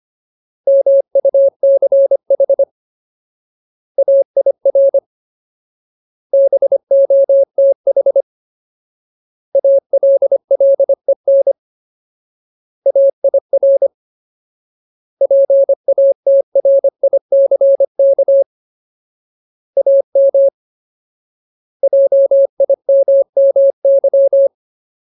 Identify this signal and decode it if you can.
Morse code, heard as a keyed tone, about 25 wpm, 550 Hz. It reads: MUCH AIR BOTH ALLEN AIR PATRICK AM JIMMY
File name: Ultimate Word List / 25WPM Ultimate